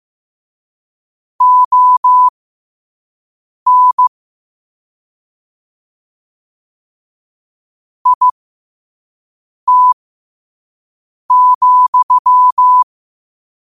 Morse code, heard as a keyed tone, about 15 wpm, spaced as for 5 wpm, 1000 Hz. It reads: ON IT,
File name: qotdfiles / 5wpm